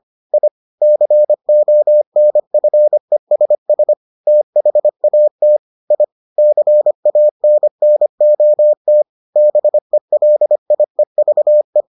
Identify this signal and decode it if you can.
Morse code, heard as a keyed tone, about 25 wpm, 600 Hz. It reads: I CONFESS THAT I CANNOT BELIEVE